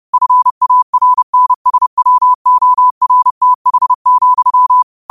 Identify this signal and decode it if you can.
Morse code, heard as a keyed tone, about 30 wpm, 1000 Hz. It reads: FARNSWORTH,